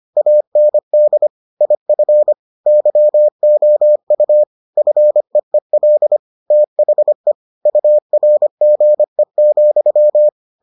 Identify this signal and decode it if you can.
Morse code, heard as a keyed tone, about 25 wpm, 600 Hz. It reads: AND IF YOU FEEL THE URGE,